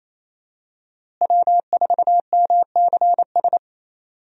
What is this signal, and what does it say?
Morse code, heard as a keyed tone, about 28 wpm, 700 Hz. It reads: W4MCS
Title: USA Call Signs - 5 Characters - Repeated 3x - 28wpm